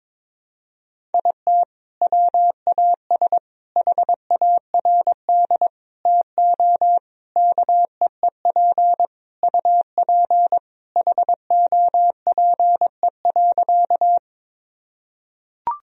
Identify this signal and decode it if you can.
Morse code, heard as a keyed tone, about 22 wpm, 700 Hz. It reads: IT WAS HARD TO KEEP UP HOPE.
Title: Sentences from Top 1000 Words - Review Part 1 - 22wpm